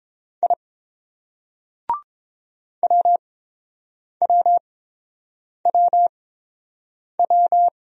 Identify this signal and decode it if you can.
Morse code, heard as a keyed tone, about 29 wpm, spaced as for 7 wpm, 700 Hz. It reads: I WWWW